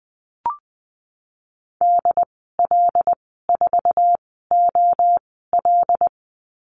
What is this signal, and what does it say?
Morse code, heard as a keyed tone, about 19 wpm, spaced as for 14 wpm, 700 Hz. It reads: DL4OL
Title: 082 Lesson 14g - Number 2 - Call Signs 20wpm